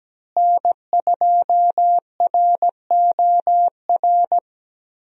Morse code, heard as a keyed tone, about 17 wpm, 700 Hz. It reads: N2ROR
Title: USA Call Signs - 5 Characters - Repeated 2x - 17wpm